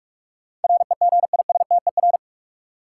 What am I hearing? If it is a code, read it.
Morse code, heard as a keyed tone, about 44 wpm, 700 Hz. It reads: REGISTER
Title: Sentences from Top 1200 Words - New Word Drill Alt - 45wpm